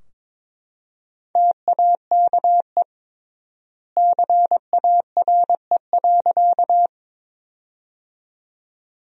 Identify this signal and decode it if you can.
Morse code, heard as a keyed tone, about 22 wpm, 700 Hz. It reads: TAKE CARE.
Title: 3x Word-Spacing - Sentences from Top 300 Words - 22wpm